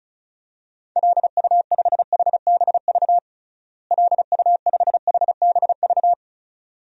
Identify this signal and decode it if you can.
Morse code, heard as a keyed tone, about 36 wpm, 700 Hz. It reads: LU5HBV LU5HBV
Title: DXCC-100 Call Signs - 6 Characters - Repeated 2x - 35wpm